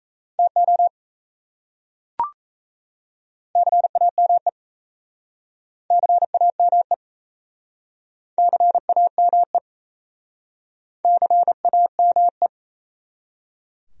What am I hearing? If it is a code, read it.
Morse code, heard as a keyed tone, about 38 wpm, 700 Hz. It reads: TO CAME CAME CAME CAME